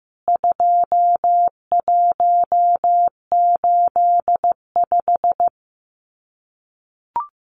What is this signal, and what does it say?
Morse code, heard as a keyed tone, about 15 wpm, 700 Hz. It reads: 2185
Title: CWT Exchange 15wpm